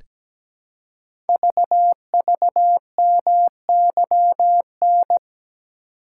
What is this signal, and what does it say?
Morse code, heard as a keyed tone, about 17 wpm, 700 Hz. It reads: VVMYN
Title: Code Groups - 5 Letters - 17wpm